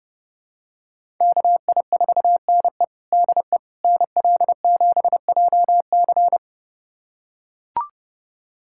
Morse code, heard as a keyed tone, about 31 wpm, 700 Hz. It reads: KI4NE DE NL7JC
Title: USA Callsign DE Callsign 30wpm